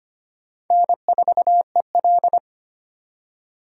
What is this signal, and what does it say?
Morse code, heard as a keyed tone, about 33 wpm, 700 Hz. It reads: N4EL